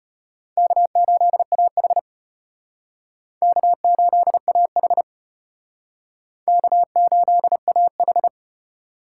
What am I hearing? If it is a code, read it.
Morse code, heard as a keyed tone, about 34 wpm, 700 Hz. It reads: K8AH K8AH K8AH